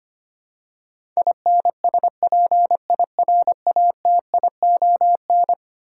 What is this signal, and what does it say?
Morse code, heard as a keyed tone, about 25 wpm, 700 Hz. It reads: INSPIRATION
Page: morse-code-ninja-practice-25wpm